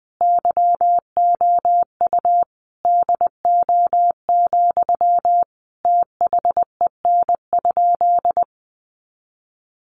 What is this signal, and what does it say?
Morse code, heard as a keyed tone, about 20 wpm, 700 Hz. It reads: YOU DO, THEN?
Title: Sentences from Top 100 Words 20wpm